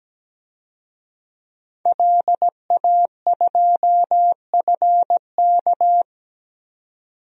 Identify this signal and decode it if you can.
Morse code, heard as a keyed tone, about 17 wpm, 700 Hz. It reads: LA2FK
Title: DXCC-100 Call Signs - 5 Characters - 17wpm